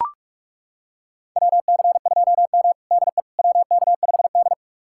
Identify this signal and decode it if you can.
Morse code, heard as a keyed tone, about 42 wpm, 700 Hz. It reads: WC2M DE WK5D
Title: USA Callsign DE Callsign - 4 Characters - 45wpm